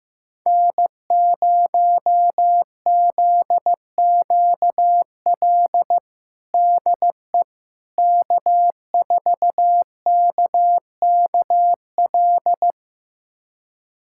Morse code, heard as a keyed tone, about 15 wpm, 700 Hz. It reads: N0ZQL DE K4KKL
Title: USA Callsign DE Callsign - 5 Characters - 15wpm